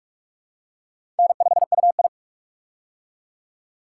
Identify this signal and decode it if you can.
Morse code, heard as a keyed tone, about 46 wpm, 700 Hz. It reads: N5UI
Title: K1USN SST QSO Practice - 45wpm